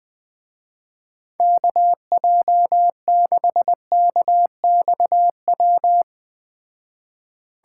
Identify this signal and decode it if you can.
Morse code, heard as a keyed tone, about 20 wpm, 700 Hz. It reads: KJ6KXW
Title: Speed Racing - USA 2x3 Call Signs 17wpm